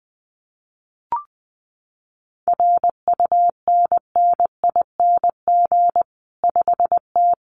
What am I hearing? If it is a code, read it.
Morse code, heard as a keyed tone, about 20 wpm, 700 Hz. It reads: RUNNING 5T
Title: Common QSO Phrases 20wpm